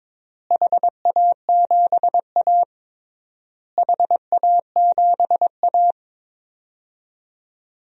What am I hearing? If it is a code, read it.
Morse code, heard as a keyed tone, about 22 wpm, 700 Hz. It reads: HA7A HA7A
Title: DXCC-100 Call Signs - 4 Characters - Repeated 3x - 22wpm